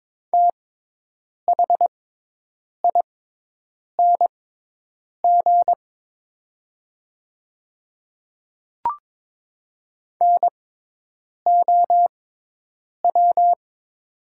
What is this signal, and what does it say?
Morse code, heard as a keyed tone, about 22 wpm, spaced as for 8 wpm, 700 Hz. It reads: THING NOW
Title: 6x Character Spacing - Top 100 Words - 22wpm